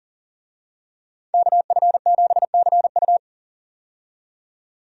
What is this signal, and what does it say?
Morse code, heard as a keyed tone, about 39 wpm, 700 Hz. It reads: KF7CU
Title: USA Call Signs - 5 Characters - 40wpm